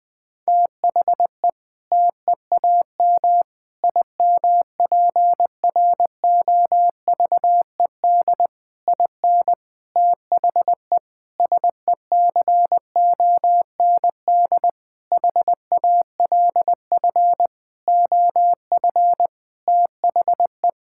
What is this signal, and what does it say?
Morse code, heard as a keyed tone, about 20 wpm, 700 Hz. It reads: THE TEAM IMPROVED IN THE SECOND HALF OF THE
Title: Sentences with 7-Word Phrases - Spoken Beginning - 20wpm